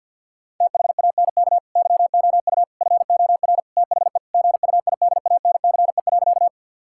Morse code, heard as a keyed tone, about 50 wpm, 700 Hz. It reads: THANK YOU FOR THE GUIDANCE.
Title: Sentences from Top 1600 Words 50wpm